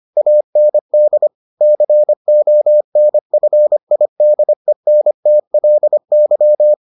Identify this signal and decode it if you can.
Morse code, heard as a keyed tone, about 25 wpm, 600 Hz. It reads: AND CONFIDENTLY